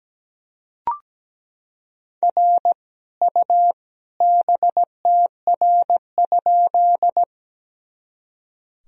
Morse code, heard as a keyed tone, about 17 wpm, 700 Hz. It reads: R U BTR?
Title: Sentences from Top 400 Words and Abbreviations 17wpm